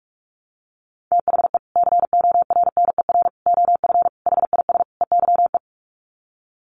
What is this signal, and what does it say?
Morse code, heard as a keyed tone, about 44 wpm, 700 Hz. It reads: THE CORNER OF HIS EYE
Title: Sets of 5 Words Encore2 45wpm